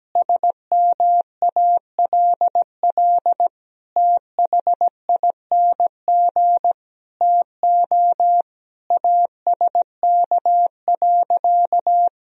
Morse code, heard as a keyed tone, about 17 wpm, 700 Hz. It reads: SMALL THING TO ASK.
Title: Sentences from Top 1000 Words - Review Part 1 - 17wpm